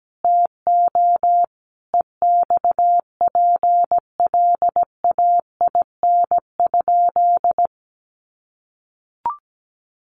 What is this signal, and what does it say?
Morse code, heard as a keyed tone, about 17 wpm, 700 Hz. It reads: TO EXPLAIN?
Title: Sentences from Top 1100 Words 17wpm